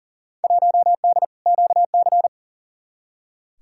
Morse code, heard as a keyed tone, about 40 wpm, 700 Hz. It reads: 1D QC
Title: Field Day Exchanges - 40wpm